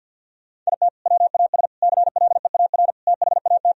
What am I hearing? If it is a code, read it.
Morse code, heard as a keyed tone, about 50 wpm, 700 Hz. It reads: IT WAS CLEAR THAT
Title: Sentences from Top 1000 Words - Review Part 4 - 50wpm